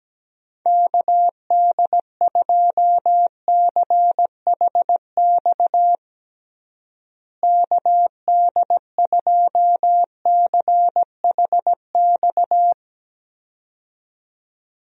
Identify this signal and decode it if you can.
Morse code, heard as a keyed tone, about 17 wpm, 700 Hz. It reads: KD2CHX KD2CHX
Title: USA Call Signs - 6 Characters - Repeated 2x - 17wpm